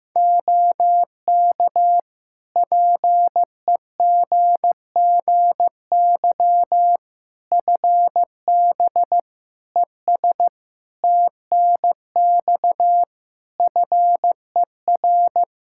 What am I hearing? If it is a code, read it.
Morse code, heard as a keyed tone, about 16 wpm, 700 Hz. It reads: OK PEGGY FB ES TNX FER
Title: Practice QSO - Exchange 2 - Info - 15wpm